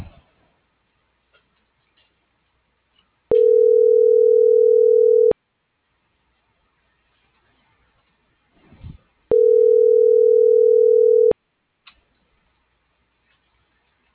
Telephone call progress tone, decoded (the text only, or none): ringback tone